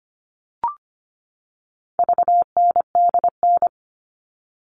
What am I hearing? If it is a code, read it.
Morse code, heard as a keyed tone, about 25 wpm, 700 Hz. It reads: VNDN
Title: All Cut Numbers - 4-Digit 25wpm